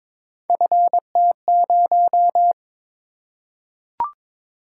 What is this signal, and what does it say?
Morse code, heard as a keyed tone, about 22 wpm, 700 Hz. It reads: FT0
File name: practice / 22wpm / Three Letter-Number Combinations - 22wpm